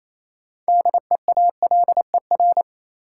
Morse code, heard as a keyed tone, about 28 wpm, 700 Hz. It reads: DEALER
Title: Sentences from Top 1100 Words - New Word Drill Alt - 28wpm